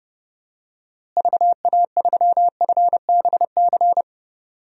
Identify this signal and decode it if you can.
Morse code, heard as a keyed tone, about 30 wpm, 700 Hz. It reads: VA3FBC
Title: DXCC-100 Call Signs - 6 Characters - Repeated 2x - 30wpm